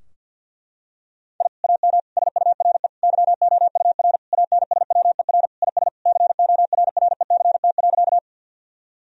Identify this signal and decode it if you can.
Morse code, heard as a keyed tone, about 49 wpm, 700 Hz. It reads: I AM SURE YOUR ANSWER IS CORRECT.
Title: Sentences from Top 1000 Words - Review Part 2 - 50wpm